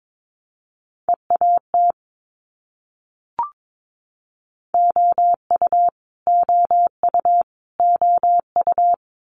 Morse code, heard as a keyed tone, about 22 wpm, 700 Hz. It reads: EAT OU OU OU
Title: Multi-letter Phonemes Repeated 3x 22wpm